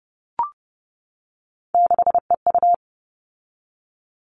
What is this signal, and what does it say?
Morse code, heard as a keyed tone, about 30 wpm, 700 Hz. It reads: 6EU
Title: All Cut Numbers - 3-Digit 30wpm